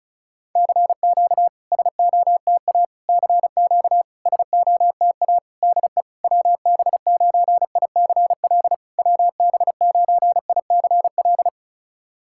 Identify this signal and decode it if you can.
Morse code, heard as a keyed tone, about 34 wpm, 690 Hz. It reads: CQ SOTA CQ SOTA DE WB9ICL WB9ICL
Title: SOTA QSO Practice 35wpm